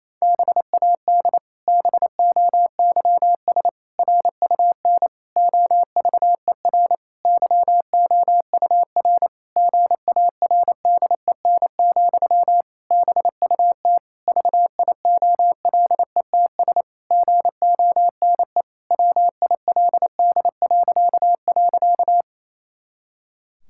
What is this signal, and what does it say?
Morse code, heard as a keyed tone, about 28 wpm, 700 Hz. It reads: BAD BOYS RUN OVER YOUR GARDEN, BUT VIOLETS GONE WILD..
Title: Pangram Challenge v2 - 28wpm